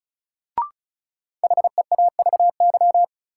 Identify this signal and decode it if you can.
Morse code, heard as a keyed tone, about 36 wpm, 700 Hz. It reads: HEAVY
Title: Sentences from Top 1000 Words - Review Part 6 - New Word Drill - 35wpm